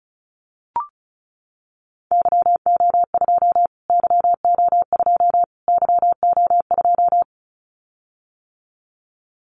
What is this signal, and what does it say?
Morse code, heard as a keyed tone, about 34 wpm, 700 Hz. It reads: YO2 YO2 YO2
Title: Callsign Prefixes for Top 25 DXCC - Repeated 3x - 35wpm